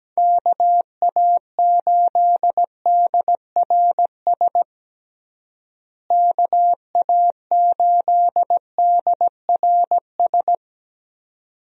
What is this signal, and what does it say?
Morse code, heard as a keyed tone, about 17 wpm, 700 Hz. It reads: KA8DRS KA8DRS
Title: USA Call Signs - 6 Characters - Repeated 2x - 17wpm